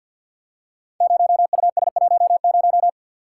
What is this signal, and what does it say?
Morse code, heard as a keyed tone, about 52 wpm, 700 Hz. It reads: 0US10
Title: Code Groups - 5 Letter:Numbers - 50wpm